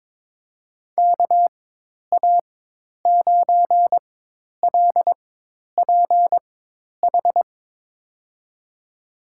Morse code, heard as a keyed tone, about 23 wpm, spaced as for 10 wpm, 700 Hz. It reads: KA9LPH
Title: USA Call Signs - 4x Character Spacing - 22wpm